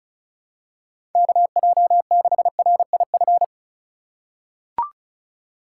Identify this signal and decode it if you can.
Morse code, heard as a keyed tone, about 36 wpm, 700 Hz. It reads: KJ6RIF
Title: USA Call Signs - 6 Characters - 3x Character Spacing - 35wpm